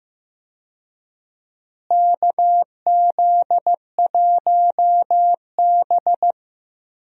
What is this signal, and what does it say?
Morse code, heard as a keyed tone, about 15 wpm, 700 Hz. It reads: KZ1B
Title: USA Call Signs - 4 Characters - Repeated 3x - 15wpm